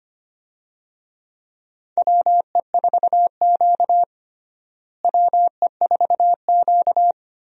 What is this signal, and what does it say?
Morse code, heard as a keyed tone, about 25 wpm, 700 Hz. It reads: WE4Q WE4Q
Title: USA Call Signs - 4 Characters - Repeated 3x - 25wpm